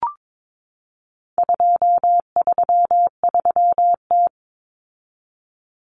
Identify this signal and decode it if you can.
Morse code, heard as a keyed tone, about 22 wpm, 700 Hz. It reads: 233T
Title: Common Cut Numbers - 4-Digit 22wpm